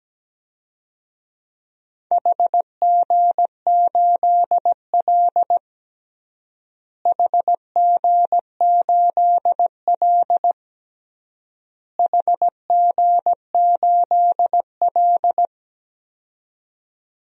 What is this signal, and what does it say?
Morse code, heard as a keyed tone, about 17 wpm, 700 Hz. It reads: HG8L HG8L HG8L
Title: DXCC-100 Call Signs - 4 Characters - Repeated 3x - 17wpm